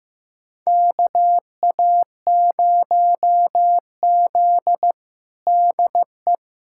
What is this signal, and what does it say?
Morse code, heard as a keyed tone, about 15 wpm, 700 Hz. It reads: KA0Z DE
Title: USA Callsign DE Callsign - 4 Characters - 15wpm